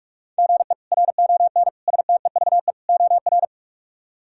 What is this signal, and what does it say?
Morse code, heard as a keyed tone, about 46 wpm, 700 Hz. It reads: GE RON STEVE OR
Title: K1USN SST QSO Practice - 45wpm